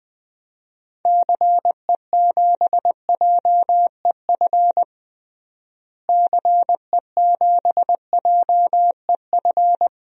Morse code, heard as a keyed tone, about 20 wpm, 700 Hz. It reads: CE7JEF CE7JEF
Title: DXCC-100 Call Signs - 6 Characters - Repeated 2x - 20wpm